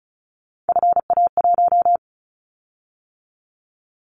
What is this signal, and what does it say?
Morse code, heard as a keyed tone, about 36 wpm, 700 Hz. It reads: FA1
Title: Callsign Prefixes for Top 25 DXCC - 35wpm